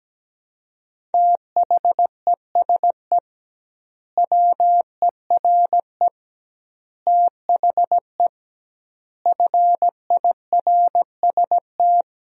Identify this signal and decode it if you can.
Morse code, heard as a keyed tone, about 17 wpm, 700 Hz. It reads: THESE WERE THE FIRST
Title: 2x Word-Spacing - Sentences from Top 300 Words - 17wpm